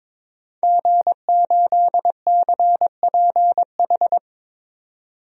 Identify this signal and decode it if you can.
Morse code, heard as a keyed tone, about 21 wpm, 700 Hz. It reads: G8CPH
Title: DXCC-100 Call Signs - 5 Characters - 22wpm